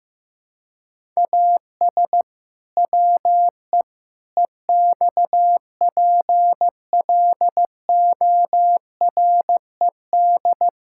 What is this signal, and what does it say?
Morse code, heard as a keyed tone, about 16 wpm, 700 Hz. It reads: AS WE EXPLORED
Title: Pangram Challenge v2 - 15wpm